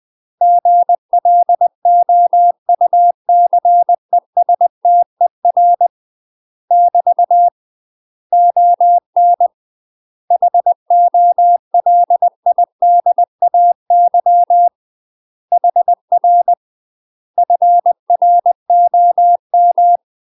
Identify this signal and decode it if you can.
Morse code, heard as a keyed tone, about 20 wpm, 700 Hz. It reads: GLOUCESTER <BT> ON HOLIDAY HR FROM
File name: UK MASTER QSOs TRAINING / 20WPM / UK MASTER QSOs 0049